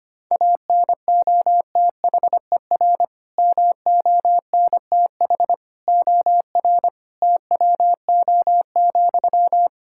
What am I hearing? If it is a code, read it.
Morse code, heard as a keyed tone, about 25 wpm, 700 Hz. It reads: ANOTHER MONTH OR TWO,
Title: Sentences from Top 1000 Words - Review Part 5 - 25wpm